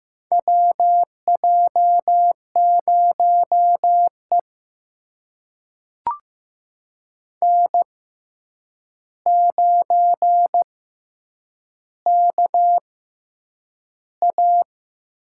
Morse code, heard as a keyed tone, about 15 wpm, spaced as for 8 wpm, 700 Hz. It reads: WJ0E N9KA
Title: USA Call Signs - 4 Characters - 6x Character Spacing - 15wpm